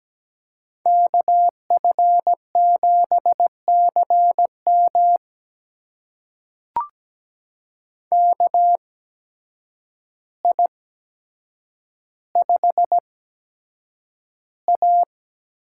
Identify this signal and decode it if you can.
Morse code, heard as a keyed tone, about 17 wpm, spaced as for 8 wpm, 700 Hz. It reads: KF7CM KI5A
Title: USA Call Signs - 8x Character Spacing - 17wpm